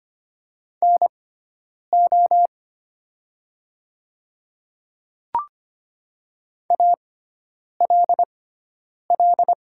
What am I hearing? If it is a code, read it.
Morse code, heard as a keyed tone, about 25 wpm, spaced as for 9 wpm, 700 Hz. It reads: NO ALL
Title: 6x Character Spacing - Top 100 Words - 25wpm